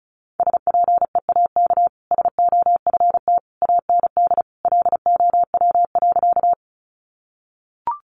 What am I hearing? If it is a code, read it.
Morse code, heard as a keyed tone, about 34 wpm, 700 Hz. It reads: SPEAK SOFT AND LOW.
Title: Sentences from Top 1500 Words 35wpm